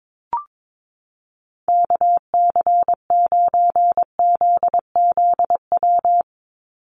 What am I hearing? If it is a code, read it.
Morse code, heard as a keyed tone, about 22 wpm, 700 Hz. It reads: KC9ZZW
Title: USA Call Signs 22wpm